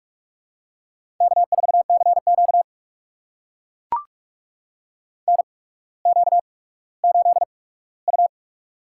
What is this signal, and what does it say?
Morse code, heard as a keyed tone, about 44 wpm, spaced as for 19 wpm, 690 Hz. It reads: K4CQ NQ8U
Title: USA Call Signs - 4 Characters - 8x Character Spacing - 45wpm